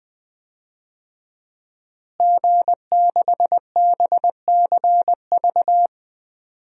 Morse code, heard as a keyed tone, about 21 wpm, 700 Hz. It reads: G6BCV
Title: DXCC-100 Call Signs - 5 Characters - Repeated 3x - 20wpm